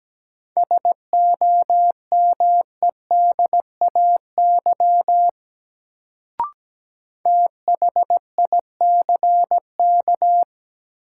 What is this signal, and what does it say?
Morse code, heard as a keyed tone, about 17 wpm, 700 Hz. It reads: SOMEDAY THICK
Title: Sentences from Top 1600 Words - New Word Drill - 17wpm